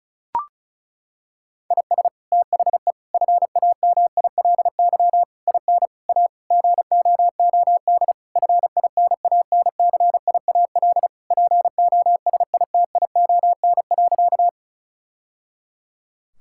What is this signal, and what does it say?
Morse code, heard as a keyed tone, about 34 wpm, 700 Hz. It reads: IS THE FAMILY IN A GOOD FINANCIAL POSITION.
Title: Sentences from Top 1000 Words - Review Part 5 - 35wpm